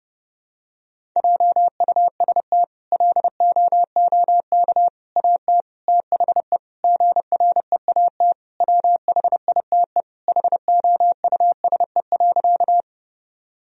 Morse code, heard as a keyed tone, about 30 wpm, 700 Hz. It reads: JUST LOOK AT THE GREAT WHITE HOUSE.
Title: Sentences from Top 1000 Words - Review Part 1 - 30wpm